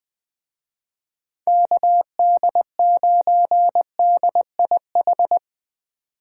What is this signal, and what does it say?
Morse code, heard as a keyed tone, about 22 wpm, 700 Hz. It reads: KD9DIH
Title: Speed Racing - USA 2x3 Call Signs 20wpm